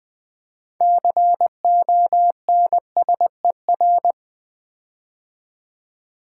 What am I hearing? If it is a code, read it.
Morse code, heard as a keyed tone, about 20 wpm, 700 Hz. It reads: CONSER
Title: Build A Word - 6 Letters At A Time - 12-Letter Words - 20wpm